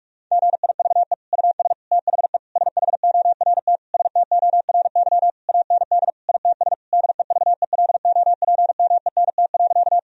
Morse code, heard as a keyed tone, about 46 wpm, 700 Hz. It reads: GIVE US THE SHORT STORY AND ITS DEVELOPMENT.
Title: Sentences from Top 1000 Words - Review Part 4 - 45wpm